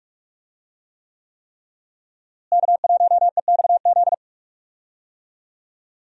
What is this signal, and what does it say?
Morse code, heard as a keyed tone, about 46 wpm, 700 Hz. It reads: K1EXZ
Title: USA Call Signs - 5 Characters - Repeated 3x - 45wpm